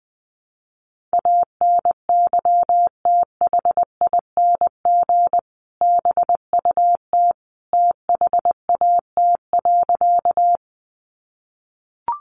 Morse code, heard as a keyed tone, about 20 wpm, 700 Hz. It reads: ANYTHING BUT THAT.
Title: Sentences from Top 1000 Words 20wpm